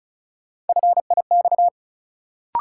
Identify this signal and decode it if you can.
Morse code, heard as a keyed tone, about 34 wpm, 700 Hz. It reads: FIX
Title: Sentences from Top 1400 Words - New Word Drill Alt - 35wpm